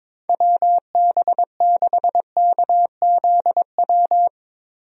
Morse code, heard as a keyed tone, about 22 wpm, 700 Hz. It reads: WB6KZW
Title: USA Call Signs - 6 Characters - Repeated 3x - 22wpm